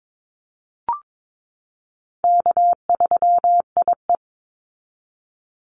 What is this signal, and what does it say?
Morse code, heard as a keyed tone, about 22 wpm, 700 Hz. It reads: K3IE
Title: Easy Call Signs Repeated 3x - 22wpm